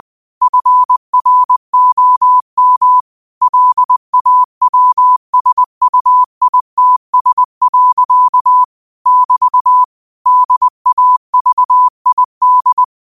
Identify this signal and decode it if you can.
Morse code, heard as a keyed tone, about 20 wpm, 1000 Hz. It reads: FROM LAWSUITS. <BT> DAVID
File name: qotdfiles / 20wpm